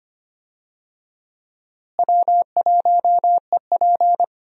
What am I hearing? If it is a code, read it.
Morse code, heard as a keyed tone, about 25 wpm, 700 Hz. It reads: W1EP
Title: USA Call Signs - 4 Characters - Repeated 3x - 25wpm